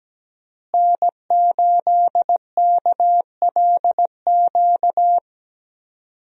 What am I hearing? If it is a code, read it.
Morse code, heard as a keyed tone, about 17 wpm, 700 Hz. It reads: N8KLQ
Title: USA Call Signs - 5 Characters - Repeated 3x - 17wpm